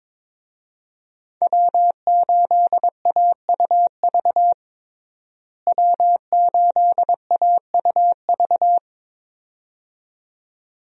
Morse code, heard as a keyed tone, about 22 wpm, 700 Hz. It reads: W8AUV W8AUV
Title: USA Call Signs - 5 Characters - Repeated 2x - 22wpm